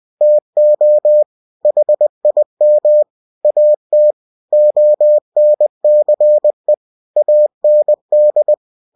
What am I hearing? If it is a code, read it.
Morse code, heard as a keyed tone, about 20 wpm, 600 Hz. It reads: TO HIM AT ONCE AND
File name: cw-books / winnie-the-pooh / winnie-20wpm-page-0098